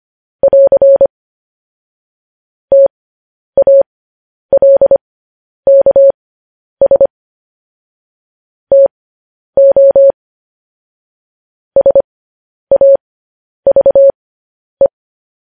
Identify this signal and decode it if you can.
Morse code, heard as a keyed tone, about 25 wpm, spaced as for 10 wpm, 550 Hz. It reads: <AR> TALKS TO SAVE
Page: news headlines in morse code at 10 wpm